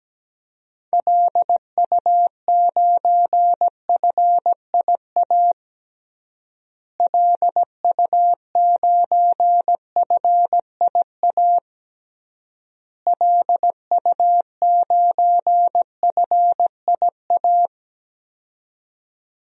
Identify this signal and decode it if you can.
Morse code, heard as a keyed tone, about 17 wpm, 700 Hz. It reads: LU9FIA LU9FIA LU9FIA
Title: DXCC-100 Call Signs - 6 Characters - Repeated 3x - 17wpm